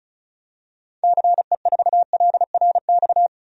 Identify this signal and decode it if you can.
Morse code, heard as a keyed tone, about 34 wpm, 700 Hz. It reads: CE4LRX
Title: DXCC-100 Call Signs - 6 Characters - 35wpm